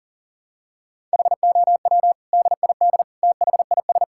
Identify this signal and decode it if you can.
Morse code, heard as a keyed tone, about 40 wpm, 700 Hz. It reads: HOW DID THIS
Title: Sentences from Top 1000 Words - Review Part 6 - 40wpm